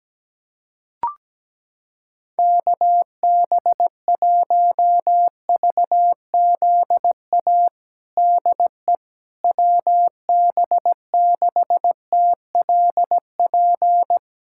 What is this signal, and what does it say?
Morse code, heard as a keyed tone, about 17 wpm, 700 Hz. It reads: KB1VZA DE WB6TLP
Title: USA Callsign DE Callsign - 6 Characters - 17wpm